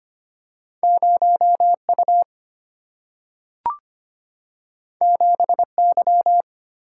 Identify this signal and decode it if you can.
Morse code, heard as a keyed tone, about 25 wpm, 700 Hz. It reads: 0U 7Y
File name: Two Letter-Number Combinations - 25wpm